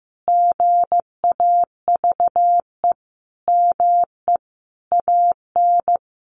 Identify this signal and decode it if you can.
Morse code, heard as a keyed tone, about 15 wpm, 700 Hz. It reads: GAVE ME AN
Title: Sentences from Top 900 Words 15wpm